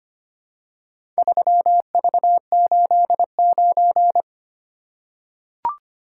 Morse code, heard as a keyed tone, about 25 wpm, 700 Hz. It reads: 3V89
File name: practice / 25wpm / Four Letter-Number Combinations - 25wpm